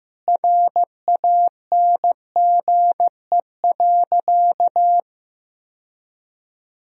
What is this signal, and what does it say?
Morse code, heard as a keyed tone, about 15 wpm, 700 Hz. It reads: RANGE.
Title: Sentences from Top 1000 Words - Review Part 4 - 15wpm